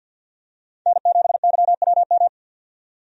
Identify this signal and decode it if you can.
Morse code, heard as a keyed tone, about 49 wpm, 700 Hz. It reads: N7YWM
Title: USA Call Signs - 5 Characters - 50wpm